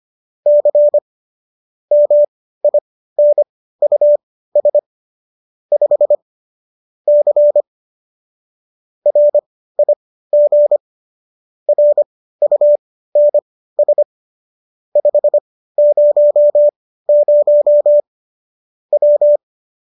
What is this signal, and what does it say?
Morse code, heard as a keyed tone, about 25 wpm, spaced as for 15 wpm, 590 Hz. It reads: C MINUS 5 C RIG RUNS 500 W